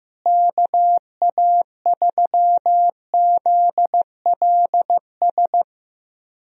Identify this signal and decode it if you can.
Morse code, heard as a keyed tone, about 15 wpm, 700 Hz. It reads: KA3ZLS
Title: USA Call Signs - 6 Characters - Repeated 3x - 15wpm